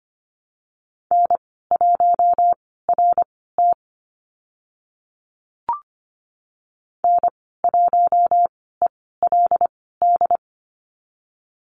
Morse code, heard as a keyed tone, about 25 wpm, spaced as for 16 wpm, 700 Hz. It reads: N1RT N1ELD
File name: Lesson 15g - Number 5 - Call Signs 25wpm